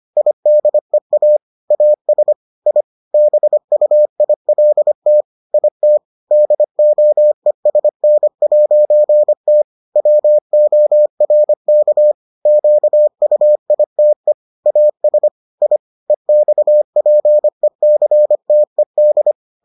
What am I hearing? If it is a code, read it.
Morse code, heard as a keyed tone, about 25 wpm, 600 Hz. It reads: IDEA AS I BUILT IT DOESN'T WORK QUITE AS I EXPECTED